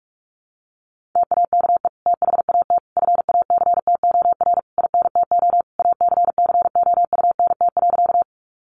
Morse code, heard as a keyed tone, about 45 wpm, 700 Hz. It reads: TAKE THAT FACTOR INTO ACCOUNT.
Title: Sentences from Top 1300 Words 45wpm